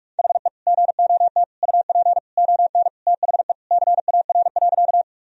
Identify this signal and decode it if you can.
Morse code, heard as a keyed tone, about 46 wpm, 700 Hz. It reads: HE GOT UP ON THE CAR.
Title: Sentences from Top 1000 Words - Review Part 1 - 45wpm